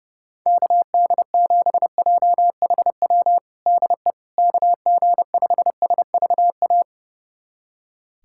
Morse code, heard as a keyed tone, about 30 wpm, 700 Hz. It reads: KD7JHW DE KG5SVA
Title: USA Callsign DE Callsign - 6 Characters - 30wpm